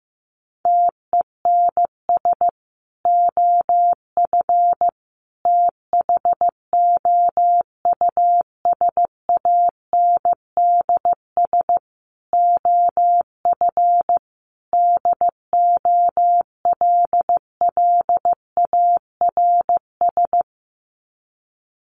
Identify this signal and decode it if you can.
Morse code, heard as a keyed tone, about 15 wpm, 700 Hz. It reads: TENS OF THOUSANDS OF DOLLARS
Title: Sets of 5 Words Encore2 15wpm